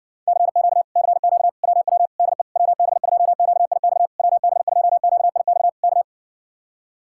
Morse code, heard as a keyed tone, about 69 wpm, 700 Hz. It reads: CQ CQ WW DE WB2QIX WB2QIX K